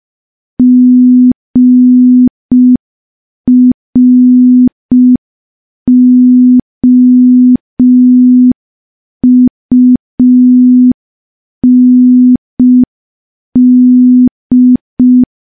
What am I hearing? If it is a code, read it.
Morse code, heard as a keyed tone, about 5 wpm, 250 Hz. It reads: GROUND
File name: morse / morse-training-05WPM